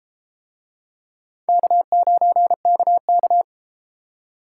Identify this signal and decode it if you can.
Morse code, heard as a keyed tone, about 32 wpm, 700 Hz. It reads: K9KK